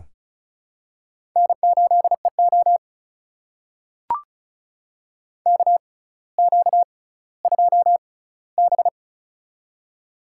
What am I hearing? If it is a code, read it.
Morse code, heard as a keyed tone, about 35 wpm, spaced as for 18 wpm, 700 Hz. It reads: N8EO KQ2B
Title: USA Call Signs - 4 Characters - 6x Character Spacing - 35wpm